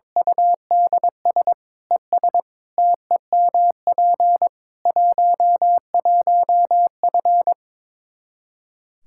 Morse code, heard as a keyed tone, about 22 wpm, 700 Hz. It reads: UDS ES TEMP 11F
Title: Practice QSO - Exchange 2 - Info - 22wpm